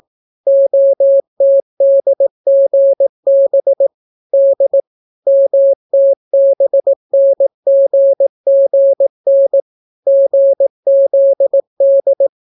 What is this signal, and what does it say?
Morse code, heard as a keyed tone, about 18 wpm, 550 Hz. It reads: OTDGB D MTBNGGN GZD